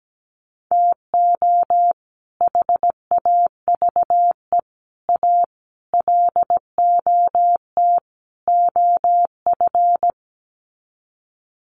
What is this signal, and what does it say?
Morse code, heard as a keyed tone, about 17 wpm, 700 Hz. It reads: TO HAVE A LOT OF
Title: Sets of 5 Words Encore 17wpm